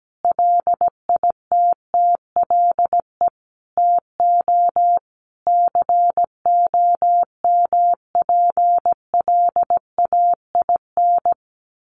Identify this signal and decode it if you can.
Morse code, heard as a keyed tone, about 17 wpm, 700 Hz. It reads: LITTLE TO COMPLAIN
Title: Sentences from Top 1300 Words 17wpm